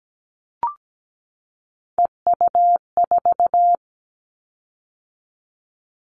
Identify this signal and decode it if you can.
Morse code, heard as a keyed tone, about 17 wpm, 700 Hz. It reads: EU4
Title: Callsign Prefixes for Top 100 DXCC - 17wpm